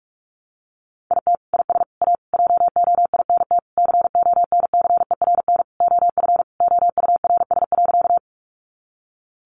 Morse code, heard as a keyed tone, about 44 wpm, 700 Hz. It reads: IT IS A JOINT CONCERN OF OURS.
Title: Sentences from Top 1300 Words 45wpm